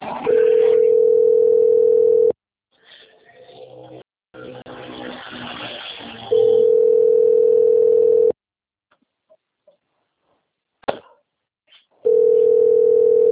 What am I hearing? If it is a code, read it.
Ringback tone (North America), 440 + 480 Hz, with speech.
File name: rg-1600-unknown- - - .6644